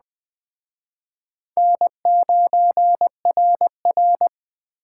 Morse code, heard as a keyed tone, about 20 wpm, 700 Hz. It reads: N9RR
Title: Speed Racing - USA 1x2 Call Signs 17wpm